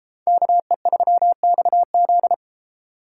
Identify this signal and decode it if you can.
Morse code, heard as a keyed tone, about 28 wpm, 700 Hz. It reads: KE3XZ